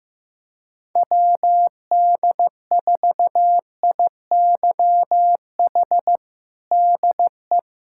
Morse code, heard as a keyed tone, about 15 wpm, 700 Hz. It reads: WD4IYH DE
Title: Practice QSO - Exchange 4 - Ending - 15wpm